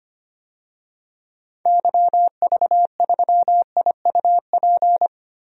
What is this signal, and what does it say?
Morse code, heard as a keyed tone, about 25 wpm, 700 Hz. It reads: YV3IUP
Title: DXCC-100 Call Signs - 6 Characters - Repeated 3x - 25wpm